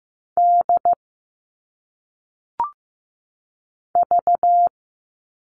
Morse code, heard as a keyed tone, about 15 wpm, 700 Hz. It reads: D V
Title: All Cut Numbers - 1-Digit 15wpm